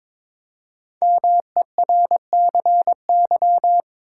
Morse code, heard as a keyed tone, about 22 wpm, 700 Hz. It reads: MERCY
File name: 5 Letter Words - First 3 Letters Given - 22wpm